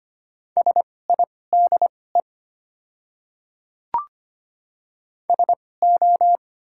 Morse code, heard as a keyed tone, about 25 wpm, spaced as for 18 wpm, 700 Hz. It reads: SIDE SO
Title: 2x Character Spacing - Top 100 Words - 25wpm